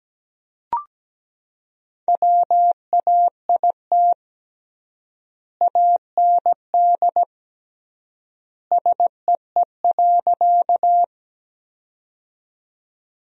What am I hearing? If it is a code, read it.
Morse code, heard as a keyed tone, about 17 wpm, 700 Hz. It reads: WAIT AND SEE.
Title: 3x Word-Spacing - Sentences from Top 500 Words - 17wpm